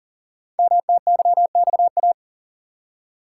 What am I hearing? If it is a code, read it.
Morse code, heard as a keyed tone, about 39 wpm, 700 Hz. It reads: MTYXA
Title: Code Groups - 5 Letters - 40wpm